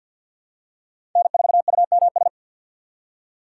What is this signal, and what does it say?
Morse code, heard as a keyed tone, about 52 wpm, 700 Hz. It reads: N4UMS
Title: USA Call Signs - 5 Characters - 3x Character Spacing - 50wpm